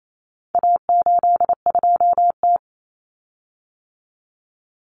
Morse code, heard as a keyed tone, about 28 wpm, 700 Hz. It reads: A82T
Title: Common Cut Numbers - 4-Digit 28wpm